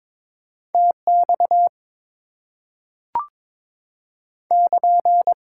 Morse code, heard as a keyed tone, about 22 wpm, 700 Hz. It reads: TX <KN>
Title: Top 90 QSO Elements 22wpm